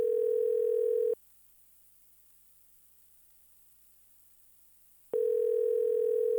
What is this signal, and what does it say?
Ringback tone (North America), 440 + 480 Hz.